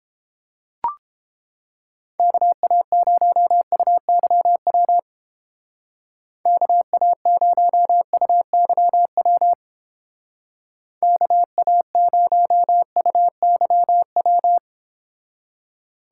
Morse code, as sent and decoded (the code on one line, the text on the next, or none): -.- .- ----- ..- -.-- .-- / -.- .- ----- ..- -.-- .-- / -.- .- ----- ..- -.-- .--
KA0UYW KA0UYW KA0UYW